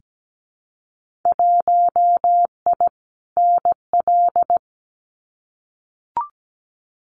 Morse code, heard as a keyed tone, about 17 wpm, 700 Hz. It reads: 1I NL
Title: Winter Field Day Exchanges Encore 17wpm